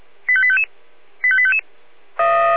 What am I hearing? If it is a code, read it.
Five-tone selective call (ZVEI), heard as quick runs of five tones, tones of about 70 ms. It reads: 85855 85855